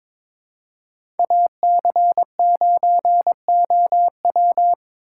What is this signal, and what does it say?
Morse code, heard as a keyed tone, about 22 wpm, 700 Hz. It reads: AC9OW
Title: USA Call Signs - 5 Characters - 2x Character Spacing - 22wpm